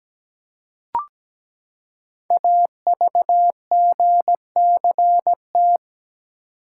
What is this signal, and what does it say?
Morse code, heard as a keyed tone, about 17 wpm, 700 Hz. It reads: AVGCT
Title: Code Groups - 5 Letter:Numbers - 17wpm